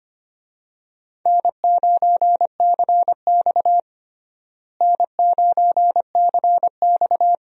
Morse code, heard as a keyed tone, about 25 wpm, 700 Hz. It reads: N9CX N9CX
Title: USA Call Signs - 4 Characters - Repeated 2x - 25wpm